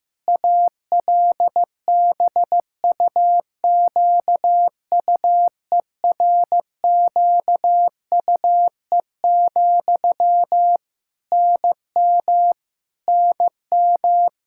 Morse code, heard as a keyed tone, about 15 wpm, 700 Hz. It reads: ALBUQUERQUE, NM NM
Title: Practice QSO - Exchange 1 - Beginning - 15wpm